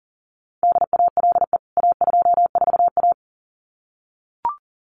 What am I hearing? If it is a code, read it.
Morse code, heard as a keyed tone, about 40 wpm, 700 Hz. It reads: DALE A24A
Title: CWT Exchange 40wpm